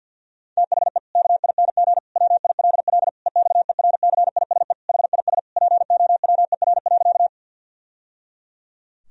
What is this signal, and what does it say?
Morse code, heard as a keyed tone, about 52 wpm, 700 Hz. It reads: THE KING WILL EXERCISE HIS POWER.
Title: Sentences from Top 1000 Words - Review Part 4 - 50wpm